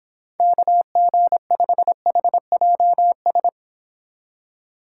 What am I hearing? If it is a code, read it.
Morse code, heard as a keyed tone, about 26 wpm, 700 Hz. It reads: KG5HJS